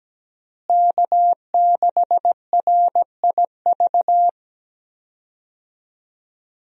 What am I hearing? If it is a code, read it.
Morse code, heard as a keyed tone, about 17 wpm, 700 Hz. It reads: K6RIV
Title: USA Call Signs - 5 Characters - Repeated 3x - 17wpm